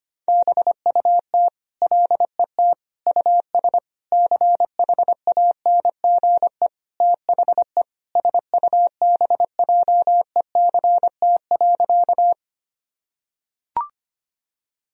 Morse code, heard as a keyed tone, about 25 wpm, 700 Hz. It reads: BUT LET US CHANGE THE SUBJECT.
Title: Sentences from Top 1000 Words - Review Part 1 - 25wpm